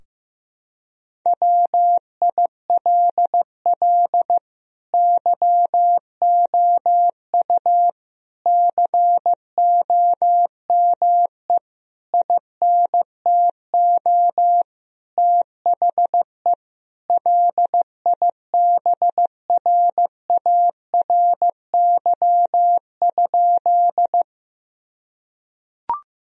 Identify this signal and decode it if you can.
Morse code, heard as a keyed tone, about 16 wpm, 700 Hz. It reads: WILL YOU COME INTO THE LIBRARY?
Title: Sentences from Top 1000 Words - Review Part 4 - 15wpm